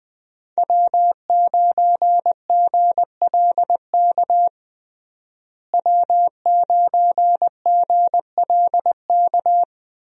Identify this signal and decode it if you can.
Morse code, heard as a keyed tone, about 21 wpm, 700 Hz. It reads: W9GLK W9GLK
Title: USA Call Signs - 5 Characters - Repeated 2x - 20wpm